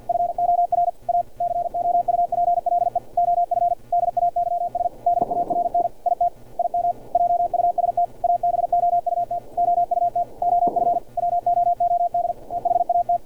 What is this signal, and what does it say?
Morse code, heard as a keyed tone, about 50 wpm, 700 Hz, with radio noise. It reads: FOR A COUPLE OF DAYS DARN IT IM JUST ABOUT OUT OF WOOD EHAT
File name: archived / CRSnet